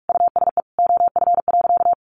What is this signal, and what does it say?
Morse code, heard as a keyed tone, about 44 wpm, 700 Hz. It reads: USE OF.
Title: Sentences from Top 1200 Words 45wpm